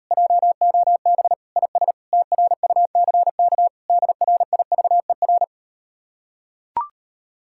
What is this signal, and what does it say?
Morse code, heard as a keyed tone, about 39 wpm, 700 Hz. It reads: JOB IS TRUCK DRIVER